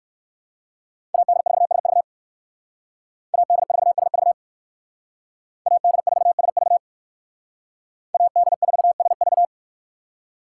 Morse code, heard as a keyed tone, about 56 wpm, 700 Hz. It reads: AD4SV AD4SV AD4SV AD4SV
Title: Speed Racing - USA Call Signs 45wpm